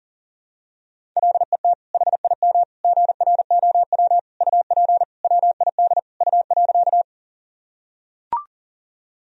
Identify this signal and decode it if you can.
Morse code, heard as a keyed tone, about 40 wpm, 700 Hz. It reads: LET HIM GROW UP WID U.
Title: Sentences from Top 300 Words and Abbreviations 40wpm